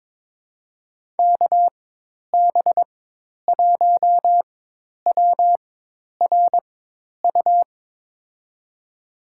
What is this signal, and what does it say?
Morse code, heard as a keyed tone, about 22 wpm, spaced as for 10 wpm, 700 Hz. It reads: KB1WRU
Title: USA Call Signs - 6 Characters - 4x Character Spacing - 22wpm